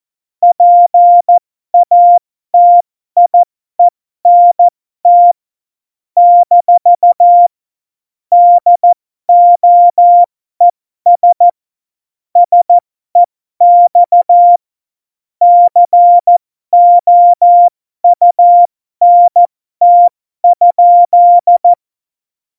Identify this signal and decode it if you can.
Morse code, heard as a keyed tone, about 14 wpm, spaced as for 12 wpm, 700 Hz. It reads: PATIENT - DOES SEX COUNT?
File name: Morsekurs / Witze